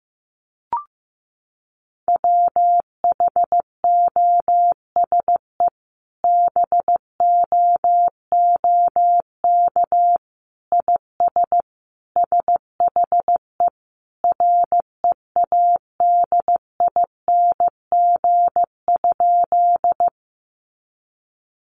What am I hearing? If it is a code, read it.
Morse code, heard as a keyed tone, about 15 wpm, 700 Hz. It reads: WHOSE BOOK IS SHE READING?
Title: Sentences from Top 1200 Words 15wpm